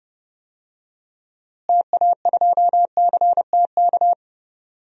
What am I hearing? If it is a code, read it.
Morse code, heard as a keyed tone, about 30 wpm, 700 Hz. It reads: TA2CTK
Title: DXCC-100 Call Signs - 6 Characters - Repeated 3x - 30wpm